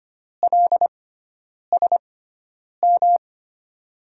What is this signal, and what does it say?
Morse code, heard as a keyed tone, about 25 wpm, spaced as for 9 wpm, 690 Hz. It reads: LSM
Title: USA Call Signs - 5 Characters - 6x Character Spacing - 25wpm